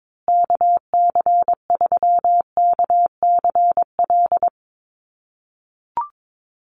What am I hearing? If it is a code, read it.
Morse code, heard as a keyed tone, about 22 wpm, 700 Hz. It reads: KC3KCL
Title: USA Call Signs Repeated 3x 22wpm